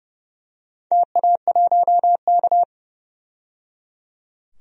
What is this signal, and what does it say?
Morse code, heard as a keyed tone, about 30 wpm, 700 Hz. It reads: TA1K
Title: DXCC-100 Call Signs - 4 Characters - 30wpm